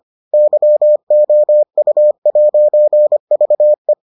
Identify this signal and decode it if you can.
Morse code, heard as a keyed tone, about 25 wpm, 600 Hz. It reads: YOU'VE